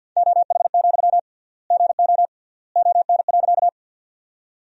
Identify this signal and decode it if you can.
Morse code, heard as a keyed tone, about 50 wpm, 700 Hz. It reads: OH, GO ON.
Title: 3x Word-Spacing - Sentences from Top 500 Words - 50wpm